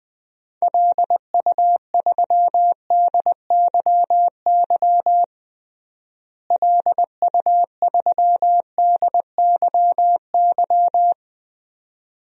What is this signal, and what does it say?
Morse code, heard as a keyed tone, about 20 wpm, 700 Hz. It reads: LU3DYY LU3DYY
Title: DXCC-100 Call Signs - 6 Characters - Repeated 2x - 20wpm